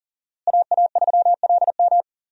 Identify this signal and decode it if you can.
Morse code, heard as a keyed tone, about 40 wpm, 700 Hz. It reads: AA3LM
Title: USA Call Signs - 5 Characters - Repeated 2x - 40wpm